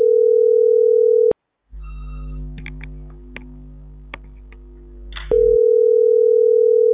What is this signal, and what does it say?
Ringback tone (North America), 440 + 480 Hz.